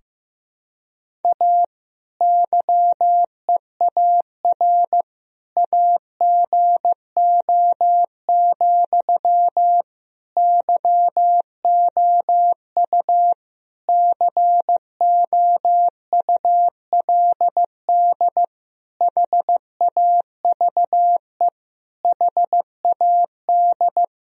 Morse code, heard as a keyed tone, about 16 wpm, 700 Hz. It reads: A YEAR AGO, YOU COULD HAVE HAD
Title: Sentences from Top 1000 Words - Review Part 3 - 15wpm